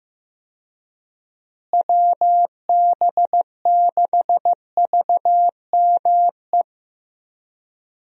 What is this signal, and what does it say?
Morse code, heard as a keyed tone, about 16 wpm, 700 Hz. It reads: WB6VME
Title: USA Call Signs - 6 Characters - Repeated 2x - 15wpm